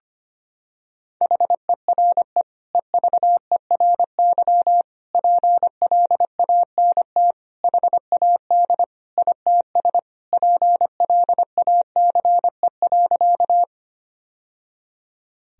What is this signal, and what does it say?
Morse code, heard as a keyed tone, about 25 wpm, 700 Hz. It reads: HERE EVERY PLANT HAD ITS PLACE.
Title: Sentences from Top 1000 Words - Review Part 1 - 25wpm